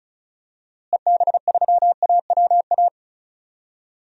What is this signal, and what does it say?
Morse code, heard as a keyed tone, about 36 wpm, 700 Hz. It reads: EB3AWA
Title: DXCC-100 Call Signs - 6 Characters - Repeated 2x - 35wpm